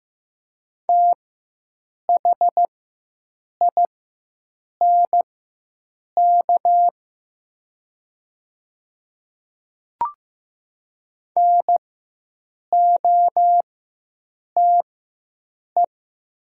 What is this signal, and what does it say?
Morse code, heard as a keyed tone, about 15 wpm, spaced as for 7 wpm, 700 Hz. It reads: THINK NOTE
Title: 4x Character Spacing - Top 500 Words - 15wpm